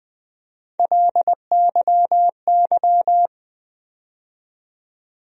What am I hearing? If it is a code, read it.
Morse code, heard as a keyed tone, about 20 wpm, 700 Hz. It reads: LYY
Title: USA Call Sign Postfixes - 3 Letters 20wpm